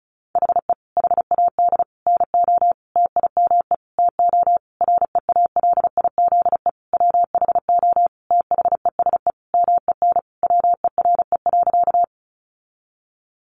Morse code, heard as a keyed tone, about 34 wpm, 700 Hz. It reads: HE HAD NO TIME TO REALIZE WHO THESE MEN WERE.
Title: Sentences from Top 1000 Words 35wpm